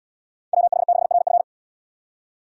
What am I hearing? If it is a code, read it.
Morse code, heard as a keyed tone, about 73 wpm, 700 Hz. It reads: WH6RF